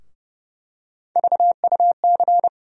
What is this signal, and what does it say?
Morse code, heard as a keyed tone, about 31 wpm, 700 Hz. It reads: VUC
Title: USA Call Sign Postfixes - 3 Letters 30wpm